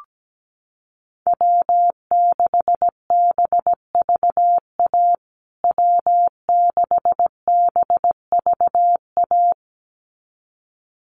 Morse code, heard as a keyed tone, about 17 wpm, 700 Hz. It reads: W6BVA W6BVA
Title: USA Call Signs Repeated 2x 17wpm